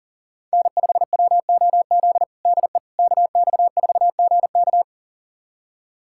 Morse code, heard as a keyed tone, about 37 wpm, 700 Hz. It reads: N5WOZ DE KX4GK
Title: USA Callsign DE Callsign - 5 Characters - 40wpm